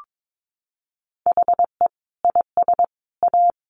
Morse code, heard as a keyed tone, about 22 wpm, 700 Hz. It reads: HE IS A
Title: Sentences from Top 1200 Words 22wpm